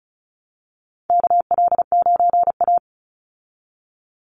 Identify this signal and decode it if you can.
Morse code, heard as a keyed tone, about 34 wpm, 700 Hz. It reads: KL9A
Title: Call Signs Repeated 2x - 35wpm